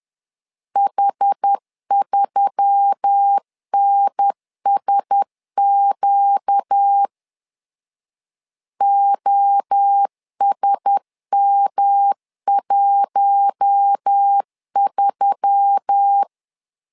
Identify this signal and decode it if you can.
Morse code, heard as a keyed tone, about 10 wpm, 800 Hz. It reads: H3NSQ OSM13